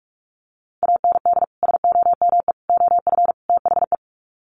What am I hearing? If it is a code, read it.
Morse code, heard as a keyed tone, about 44 wpm, 700 Hz. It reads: AND SOME OF THE
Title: Sets of 4 Words 45wpm